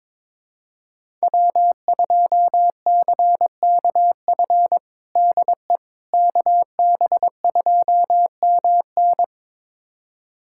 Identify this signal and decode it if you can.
Morse code, heard as a keyed tone, about 21 wpm, 700 Hz. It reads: W2CKF DE KB2MN
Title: USA Callsign DE Callsign - 5 Characters - 22wpm